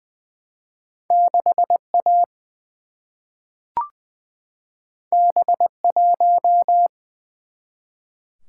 Morse code, heard as a keyed tone, about 20 wpm, 700 Hz. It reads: 6 A B 1
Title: Single-Letter Single-Number 20wpm